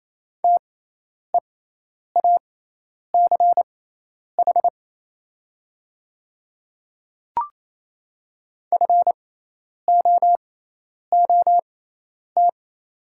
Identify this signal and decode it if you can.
Morse code, heard as a keyed tone, about 28 wpm, spaced as for 10 wpm, 700 Hz. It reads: TEACH FOOT
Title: 6x Character Spacing - Top 500 Words - 28wpm